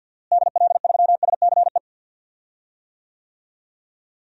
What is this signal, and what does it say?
Morse code, heard as a keyed tone, about 46 wpm, 700 Hz. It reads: DL3SCE
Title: DXCC-100 Call Signs - 6 Characters - Repeated 3x - 50wpm